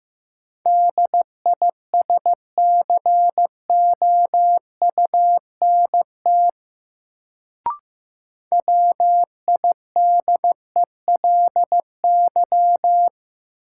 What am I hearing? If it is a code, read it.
Morse code, heard as a keyed tone, about 16 wpm, 700 Hz. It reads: DISCOUNT WIDELY
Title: Sentences from Top 1200 Words - New Word Drill Alt - 15wpm